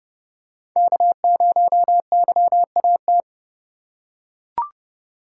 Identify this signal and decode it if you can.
Morse code, heard as a keyed tone, about 30 wpm, 700 Hz. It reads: K0YAT
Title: USA Call Signs - 5 Characters - 2x Character Spacing - 30wpm